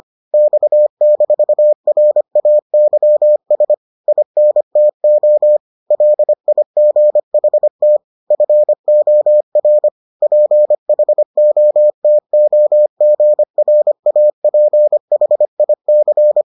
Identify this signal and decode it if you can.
Morse code, heard as a keyed tone, about 25 wpm, 600 Hz. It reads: X-RAYS INTO LIGHT FOR PHOTOGRAPHIC